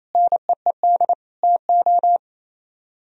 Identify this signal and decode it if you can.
Morse code, heard as a keyed tone, about 38 wpm, 700 Hz. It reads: NEED TO